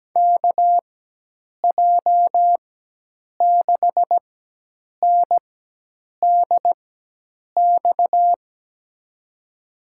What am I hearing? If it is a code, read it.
Morse code, heard as a keyed tone, about 17 wpm, spaced as for 8 wpm, 700 Hz. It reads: KJ6NDX
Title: USA Call Signs - 4x Character Spacing - 17wpm